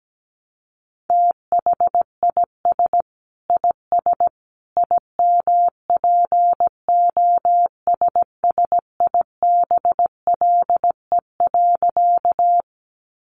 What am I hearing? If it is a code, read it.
Morse code, heard as a keyed tone, about 17 wpm, 700 Hz. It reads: THIS IS IMPOSSIBLE.
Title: A Sound Of Thunder 17wpm